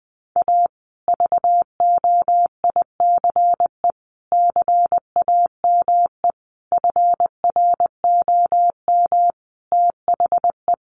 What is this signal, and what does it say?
Morse code, heard as a keyed tone, about 20 wpm, 700 Hz. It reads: A VOICE CAME FROM THE
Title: Sentences from Top 1200 Words 20wpm